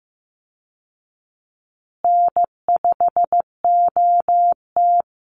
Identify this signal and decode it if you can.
Morse code, heard as a keyed tone, about 15 wpm, 700 Hz. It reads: N5OT
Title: Easy Call Signs Repeated 3x - 15wpm